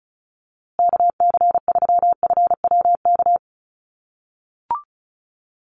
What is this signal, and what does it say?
Morse code, heard as a keyed tone, about 34 wpm, 700 Hz. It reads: KC3FWK
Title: USA Call Signs Repeated 2x 35wpm